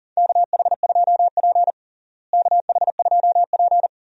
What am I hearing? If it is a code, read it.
Morse code, heard as a keyed tone, about 37 wpm, 700 Hz. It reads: KH2P KH2P
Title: USA Call Signs - 4 Characters - Repeated 2x - 40wpm